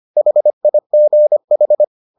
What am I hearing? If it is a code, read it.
Morse code, heard as a keyed tone, about 25 wpm, 600 Hz. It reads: HIGH